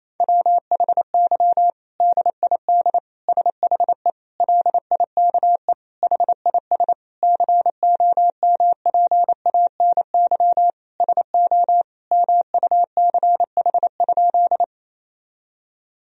Morse code, heard as a keyed tone, about 27 wpm, 700 Hz. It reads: WHY DID SHE LIKE HIS COMPANY SO MUCH?
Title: Sentences from Top 1000 Words - Review Part 3 - 28wpm